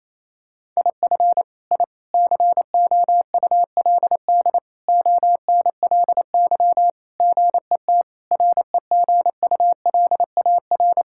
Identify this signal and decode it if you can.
Morse code, heard as a keyed tone, about 28 wpm, 700 Hz. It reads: IF I COULD ONLY GET REGULAR
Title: Sentences from Top 1000 Words - Review Part 6 - 28wpm